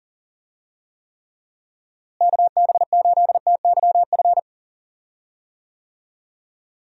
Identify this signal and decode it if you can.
Morse code, heard as a keyed tone, about 40 wpm, 700 Hz. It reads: KB8TYF
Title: USA Call Signs - 6 Characters - Repeated 2x - 40wpm